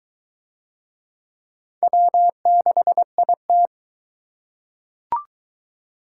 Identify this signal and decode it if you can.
Morse code, heard as a keyed tone, about 21 wpm, 700 Hz. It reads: W6IT